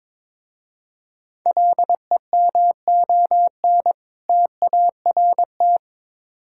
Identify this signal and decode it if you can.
Morse code, heard as a keyed tone, about 22 wpm, 700 Hz. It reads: LEMON TART
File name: Difficult Words with Hints - 22wpm